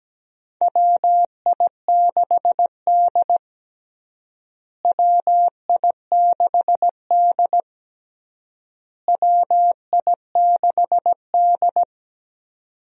Morse code, heard as a keyed tone, about 17 wpm, 700 Hz. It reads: WI6D WI6D WI6D
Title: USA Call Signs - 4 Characters - Repeated 3x - 17wpm